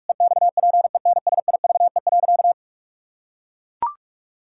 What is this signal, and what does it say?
Morse code, heard as a keyed tone, about 41 wpm, 700 Hz. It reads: EXPENSIVE.
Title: Pangram Challenge v2 - 45wpm